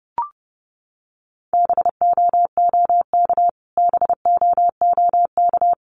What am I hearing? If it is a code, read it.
Morse code, heard as a keyed tone, about 30 wpm, 700 Hz. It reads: BOOK BOOK
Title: Top 500 Words Repeated 30wpm